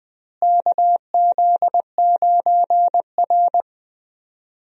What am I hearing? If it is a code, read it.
Morse code, heard as a keyed tone, about 20 wpm, 700 Hz. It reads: KZ9R
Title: USA Call Signs - 4 Characters - Repeated 3x - 20wpm